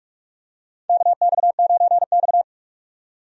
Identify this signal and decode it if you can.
Morse code, heard as a keyed tone, about 44 wpm, 700 Hz. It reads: KX9X
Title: USA Call Signs - 4 Characters - 45wpm